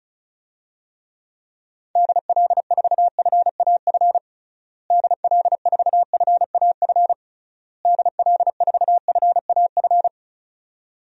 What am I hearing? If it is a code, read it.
Morse code, heard as a keyed tone, about 36 wpm, 700 Hz. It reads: DL4FAF DL4FAF DL4FAF
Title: DXCC-100 Call Signs - 6 Characters - Repeated 3x - 35wpm